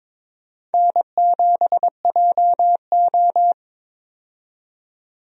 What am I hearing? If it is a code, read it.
Morse code, heard as a keyed tone, about 22 wpm, 700 Hz. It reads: N7JO
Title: USA Call Signs - 4 Characters - 22wpm